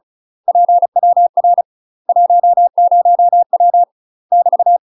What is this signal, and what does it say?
Morse code, heard as a keyed tone, about 36 wpm, 700 Hz. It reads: PWR 10W <BT>
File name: UK MASTER QSOs TRAINING / 35WPM / UK MASTER QSOs 0030